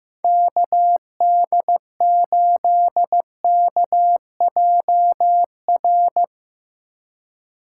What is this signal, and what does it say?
Morse code, heard as a keyed tone, about 16 wpm, 700 Hz. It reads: KD8KJR
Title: Speed Racing - USA 2x3 Call Signs 15wpm